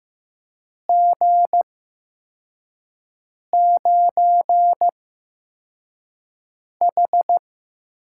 Morse code, heard as a keyed tone, about 15 wpm, spaced as for 4 wpm, 700 Hz. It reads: G9H
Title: USA Call Signs - 4 Characters - 8x Character Spacing - 15wpm